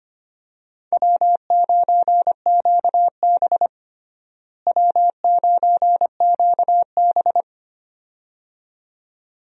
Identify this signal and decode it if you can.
Morse code, heard as a keyed tone, about 25 wpm, 700 Hz. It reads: W9QB W9QB
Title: USA Call Signs - 4 Characters - Repeated 2x - 25wpm